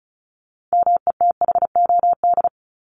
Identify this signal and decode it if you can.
Morse code, heard as a keyed tone, about 34 wpm, 700 Hz. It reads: METHOD
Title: Top 501-750 Words 35wpm